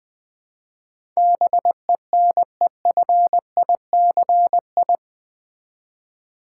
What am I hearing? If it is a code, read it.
Morse code, heard as a keyed tone, about 20 wpm, 700 Hz. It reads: BENEFICI
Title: Build A Word - 4 Letters At A Time - 11-Letter Words - 20wpm